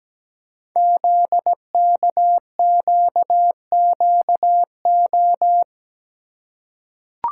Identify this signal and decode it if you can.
Morse code, heard as a keyed tone, about 17 wpm, 700 Hz. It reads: ZKQQO
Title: Code Groups - 5 Letters - 17wpm